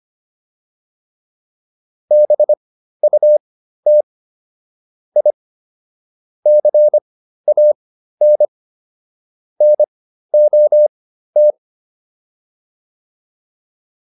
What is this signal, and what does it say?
Morse code, heard as a keyed tone, about 25 wpm, spaced as for 13 wpm, 600 Hz. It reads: BUT I CAN NOT